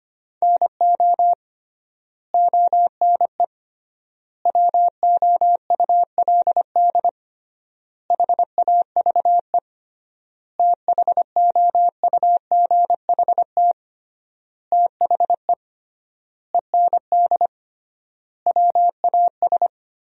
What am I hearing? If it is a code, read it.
Morse code, heard as a keyed tone, about 25 wpm, 700 Hz. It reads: NO ONE WOULD HAVE THOUGHT THE END WAS
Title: 3x Word-Spacing - Sentences from Top 300 Words - 25wpm